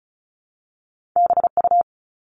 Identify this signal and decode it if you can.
Morse code, heard as a keyed tone, about 36 wpm, 700 Hz. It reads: BU
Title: All Cut Numbers - 2-Digit 35wpm